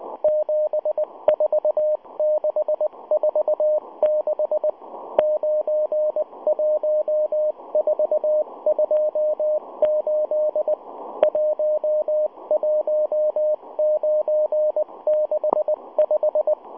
Morse code, heard as a keyed tone, about 19 wpm, 610 Hz. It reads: 74646 91428 11965